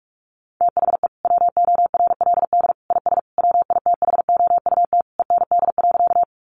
Morse code, heard as a keyed tone, about 46 wpm, 700 Hz. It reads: THE WORLD IS WITHOUT END.
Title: Sentences from Top 600 Words 45wpm